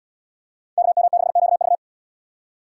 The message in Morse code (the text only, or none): KM6QV